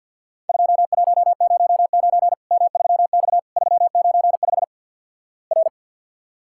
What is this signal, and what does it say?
Morse code, heard as a keyed tone, about 48 wpm, 700 Hz. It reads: 2109 M3X 395 R
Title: Worked All Europe DX Contest - QTC Exchange - 50wpm